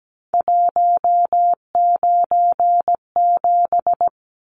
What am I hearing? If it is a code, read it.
Morse code, heard as a keyed tone, about 17 wpm, 700 Hz. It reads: 197
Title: Numbers - 3-Digit 17wpm